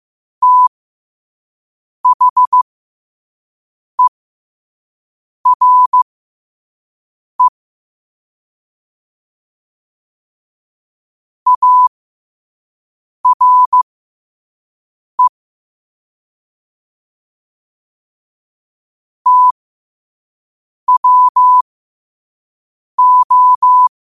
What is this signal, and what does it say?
Morse code, heard as a keyed tone, about 15 wpm, spaced as for 5 wpm, 1000 Hz. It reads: THERE ARE TWO
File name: qotdfiles / 5wpm